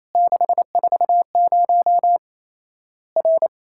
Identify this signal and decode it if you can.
Morse code, heard as a keyed tone, about 28 wpm, 700 Hz. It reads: 640 R
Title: Worked All Europe DX Contest - QTC Exchange - 28wpm